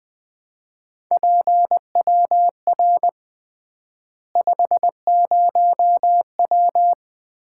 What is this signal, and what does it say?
Morse code, heard as a keyed tone, about 20 wpm, 700 Hz. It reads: PWR 50W
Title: Common QSO Phrases - 3x Word-Spacing - 20wpm